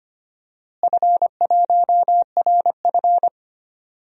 Morse code, heard as a keyed tone, about 25 wpm, 700 Hz. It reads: F1RF
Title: DXCC-100 Call Signs - 4 Characters - Repeated 3x - 25wpm